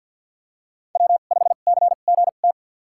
Morse code, heard as a keyed tone, about 48 wpm, spaced as for 35 wpm, 690 Hz. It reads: W5CGT
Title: USA Call Signs - 5 Characters - 2x Character Spacing - 50wpm